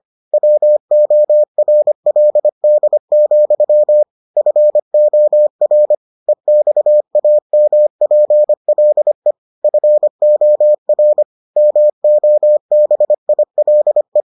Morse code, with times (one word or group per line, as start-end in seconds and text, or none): 0.3-4.0 WORLD,
4.4-6.0 FOR
6.3-9.3 EXAMPLE
9.6-11.2 FOR
11.6-14.2 MOBILE